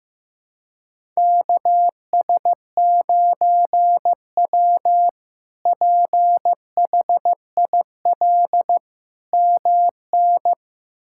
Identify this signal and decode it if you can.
Morse code, heard as a keyed tone, about 15 wpm, 700 Hz. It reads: KS9W PHIL MN
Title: K1USN SST QSO Practice - 15wpm